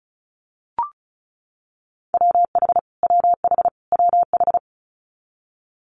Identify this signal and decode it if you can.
Morse code, heard as a keyed tone, about 36 wpm, 700 Hz. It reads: WH WH WH
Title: Multi-letter Phonemes Repeated 3x 35wpm